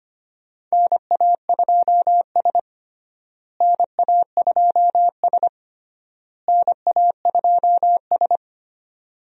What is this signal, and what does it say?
Morse code, heard as a keyed tone, about 25 wpm, 700 Hz. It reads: NA2S NA2S NA2S
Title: USA Call Signs - 4 Characters - Repeated 3x - 25wpm